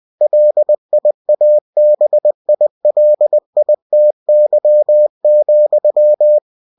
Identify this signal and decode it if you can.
Morse code, heard as a keyed tone, about 20 wpm, 600 Hz. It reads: LIABILITY,
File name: cw-books / winnie-the-pooh / winnie-20wpm-page-0120